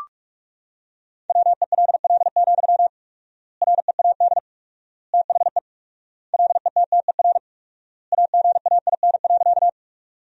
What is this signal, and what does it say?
Morse code, heard as a keyed tone, about 46 wpm, 700 Hz. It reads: WELL, READ THE LETTER AGAIN.
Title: 4x Word-Spacing - Sentences from Top 300 Words - 45wpm